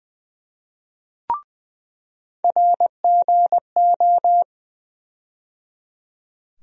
Morse code, heard as a keyed tone, about 20 wpm, 700 Hz. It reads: RGO
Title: USA Call Sign Postfixes - 3 Letters 20wpm